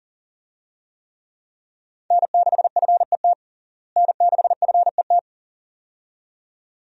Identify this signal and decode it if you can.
Morse code, heard as a keyed tone, about 40 wpm, 700 Hz. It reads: N6FET N6FET
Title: USA Call Signs - 5 Characters - Repeated 2x - 40wpm